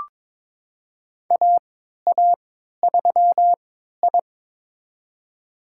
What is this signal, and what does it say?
Morse code, heard as a keyed tone, about 22 wpm, spaced as for 12 wpm, 700 Hz. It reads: AA3I
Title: USA Call Signs - 3x Character Spacing - 22wpm